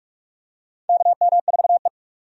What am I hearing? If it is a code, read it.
Morse code, heard as a keyed tone, about 46 wpm, 700 Hz. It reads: KM4E
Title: USA Call Signs - 4 Characters - 45wpm